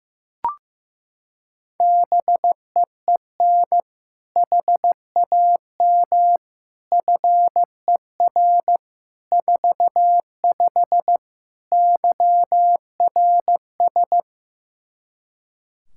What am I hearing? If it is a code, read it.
Morse code, heard as a keyed tone, about 15 wpm, 700 Hz. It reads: BEEN HAM FER 45 YRS
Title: Practice QSO - Exchange 3 - More Info - 15wpm